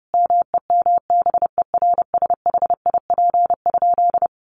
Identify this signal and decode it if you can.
Morse code, heard as a keyed tone, about 30 wpm, 700 Hz. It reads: MEMBERSHIP?
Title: Sentences from Top 1400 Words 30wpm